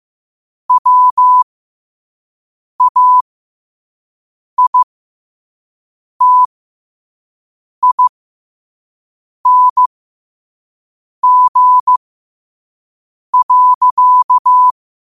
Morse code, heard as a keyed tone, about 15 wpm, spaced as for 5 wpm, 1000 Hz. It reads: WAITING.
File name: qotdfiles / 5wpm